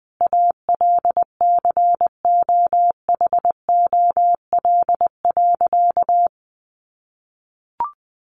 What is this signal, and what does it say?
Morse code, heard as a keyed tone, about 20 wpm, 700 Hz. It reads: ALCOHOL.
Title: Sentences from Top 1400 Words 20wpm